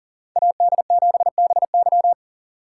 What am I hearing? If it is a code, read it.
Morse code, heard as a keyed tone, about 40 wpm, 700 Hz. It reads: AD7BY
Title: USA Call Signs - 3x Character Spacing - 40wpm